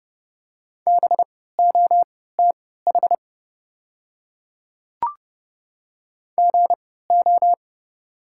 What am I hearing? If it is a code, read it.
Morse code, heard as a keyed tone, about 31 wpm, spaced as for 17 wpm, 690 Hz. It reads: BOTH GO
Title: 3x Character Spacing - Top 300 Words - 30wpm